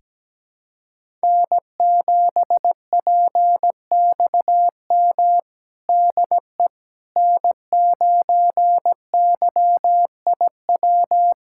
Morse code, heard as a keyed tone, about 17 wpm, 700 Hz. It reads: N7PXM DE N9YIW
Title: USA Callsign DE Callsign - 5 Characters - 17wpm